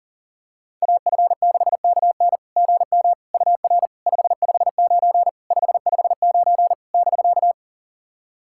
Morse code, heard as a keyed tone, about 40 wpm, 700 Hz. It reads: AF6KN GM UR 559 559 <BK>
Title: SOTA QSO Practice 40wpm